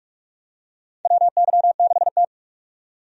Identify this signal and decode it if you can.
Morse code, heard as a keyed tone, about 46 wpm, 700 Hz. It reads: WY6T
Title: USA Call Signs - 4 Characters - 45wpm